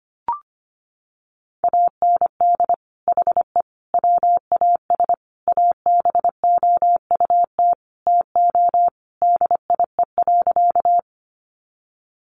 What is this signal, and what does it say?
Morse code, heard as a keyed tone, about 25 wpm, 700 Hz. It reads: AND HE WAS ABOUT TO DIE.
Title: Sentences from Top 1300 Words 25wpm